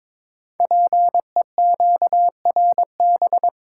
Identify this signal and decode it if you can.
Morse code, heard as a keyed tone, about 22 wpm, 700 Hz. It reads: PEQRB
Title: Code Groups - 5 Letters - 22wpm